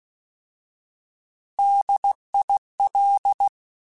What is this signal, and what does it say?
Morse code, heard as a keyed tone, about 16 wpm, 800 Hz. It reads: DIL
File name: Alphabetical / Alphabetical mp3s / Ds @15